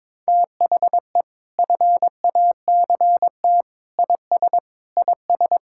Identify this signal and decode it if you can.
Morse code, heard as a keyed tone, about 22 wpm, 700 Hz. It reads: THE FACT IS IS
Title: Sentences from Top 1600 Words 22wpm